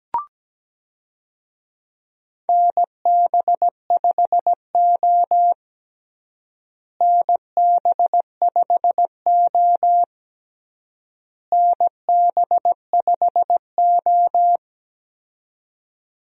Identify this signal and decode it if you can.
Morse code, heard as a keyed tone, about 17 wpm, 700 Hz. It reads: NB5O NB5O NB5O
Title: USA Call Signs - 4 Characters - Repeated 3x - 17wpm